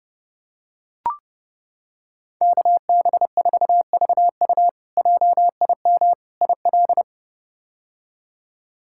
Morse code, heard as a keyed tone, about 30 wpm, 700 Hz. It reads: KB4VU JIM IL
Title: K1USN SST QSO Practice - 30wpm